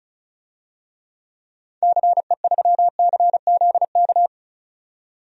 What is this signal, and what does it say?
Morse code, heard as a keyed tone, about 36 wpm, 700 Hz. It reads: CE3CZK
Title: DXCC-100 Call Signs - 6 Characters - 35wpm